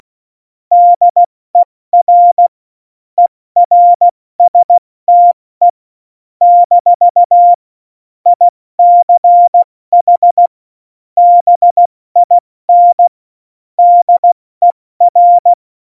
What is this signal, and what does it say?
Morse code, heard as a keyed tone, about 16 wpm, spaced as for 14 wpm, 700 Hz. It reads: DER ERSTE - ICH BIN DER
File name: Morsekurs / Witze / Witz 01. Dezember 2020, 70 F80